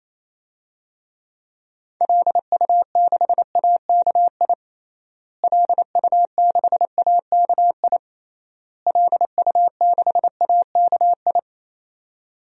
Morse code, heard as a keyed tone, about 28 wpm, 700 Hz. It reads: LU6AKI LU6AKI LU6AKI
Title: DXCC-100 Call Signs - 6 Characters - Repeated 3x - 28wpm